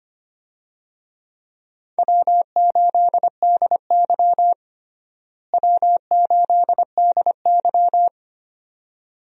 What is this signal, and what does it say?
Morse code, heard as a keyed tone, about 25 wpm, 700 Hz. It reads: W8DY W8DY
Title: USA Call Signs - 4 Characters - Repeated 2x - 25wpm